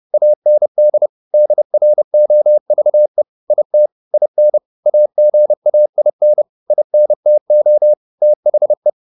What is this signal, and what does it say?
Morse code, heard as a keyed tone, about 30 wpm, 600 Hz. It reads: AND DROVE IT IN AGAIN INTO THE